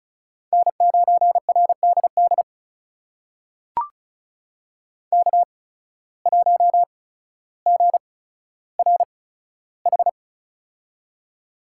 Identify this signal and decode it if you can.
Morse code, heard as a keyed tone, about 36 wpm, spaced as for 15 wpm, 700 Hz. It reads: N9RDD K1GRH
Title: USA Call Signs - 5 Characters - 8x Character Spacing - 35wpm